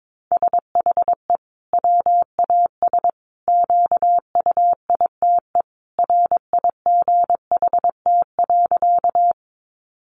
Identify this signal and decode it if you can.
Morse code, heard as a keyed tone, about 22 wpm, 700 Hz. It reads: SHE WAS QUITE RIGHT.
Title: Sentences from Top 700 Words 22wpm